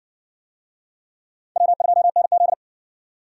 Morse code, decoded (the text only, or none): W2NZ